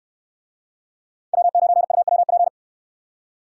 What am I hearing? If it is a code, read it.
Morse code, heard as a keyed tone, about 68 wpm, 700 Hz. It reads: W9UWP